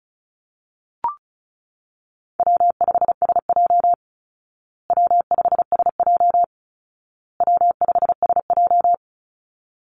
Morse code, heard as a keyed tone, about 34 wpm, 700 Hz. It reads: W5SJ W5SJ W5SJ
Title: Call Signs Repeated 2x - 35wpm